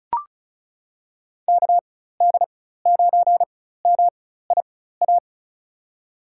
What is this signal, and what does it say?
Morse code, heard as a keyed tone, about 36 wpm, spaced as for 16 wpm, 700 Hz. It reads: KD9MIA
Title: USA Call Signs - 6 Characters - 4x Character Spacing - 35wpm